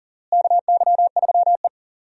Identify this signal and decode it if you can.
Morse code, heard as a keyed tone, about 40 wpm, 700 Hz. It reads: KY3E
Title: USA Call Signs - 4 Characters - Repeated 2x - 40wpm